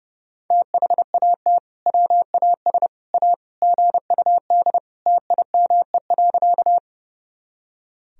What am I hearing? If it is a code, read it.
Morse code, heard as a keyed tone, about 30 wpm, 700 Hz. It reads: THAT WAS A GUD TIME.
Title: Sentences from Top 200 Words and Abbreviations 30wpm